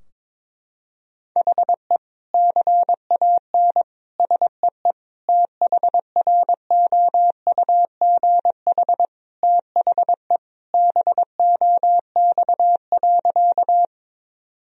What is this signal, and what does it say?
Morse code, heard as a keyed tone, about 22 wpm, 700 Hz. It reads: HE CAN SEE THROUGH THE BOX.
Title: Sentences from Top 1000 Words - Review Part 2 - 22wpm